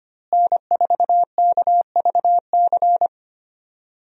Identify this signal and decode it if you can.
Morse code, heard as a keyed tone, about 25 wpm, 700 Hz. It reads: N4KVC
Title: USA Call Signs - 5 Characters - Repeated 3x - 25wpm